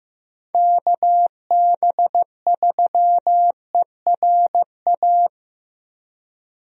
Morse code, heard as a keyed tone, about 15 wpm, 700 Hz. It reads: KB3ERA
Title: USA Call Signs - 6 Characters - 15wpm